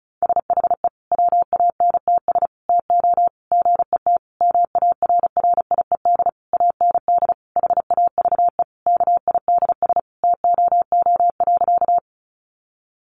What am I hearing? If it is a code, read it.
Morse code, heard as a keyed tone, about 34 wpm, 700 Hz. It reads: SHE WANTS TO GET MARRIED AND HAVE KIDS TOO.
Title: Easy Phrases 35wpm